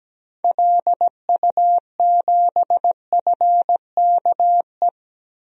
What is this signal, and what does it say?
Morse code, heard as a keyed tone, about 17 wpm, 700 Hz. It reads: LU7FKE
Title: DXCC-100 Call Signs - 6 Characters - 17wpm